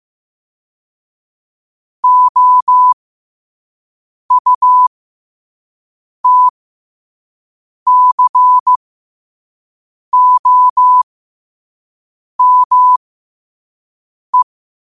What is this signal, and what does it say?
Morse code, heard as a keyed tone, about 15 wpm, spaced as for 5 wpm, 1000 Hz. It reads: OUTCOME
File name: qotdfiles / 5wpm